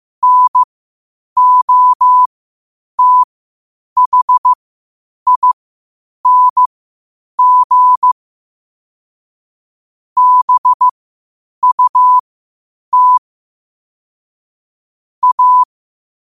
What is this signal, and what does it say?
Morse code, heard as a keyed tone, about 15 wpm, spaced as for 8 wpm, 1000 Hz. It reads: NOTHING BUT A
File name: qotdfiles / 8wpm